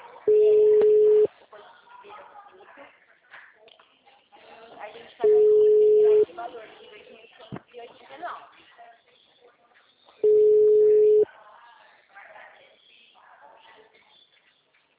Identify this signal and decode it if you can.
Ringback tone (Europe, 425 Hz), with speech.
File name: out- -2008- - - .63